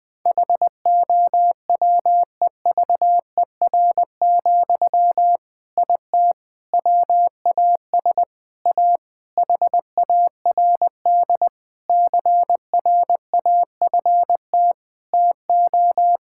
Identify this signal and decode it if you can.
Morse code, heard as a keyed tone, about 20 wpm, 700 Hz. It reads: HOWEVER, IT WAS A HARD CRAFT TO
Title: Sentences from Top 1000 Words - Review Part 6 - 20wpm